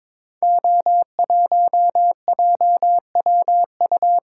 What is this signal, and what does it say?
Morse code, heard as a keyed tone, about 22 wpm, 700 Hz. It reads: O1JWU
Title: Code Groups - 5 Letter:Numbers - 22wpm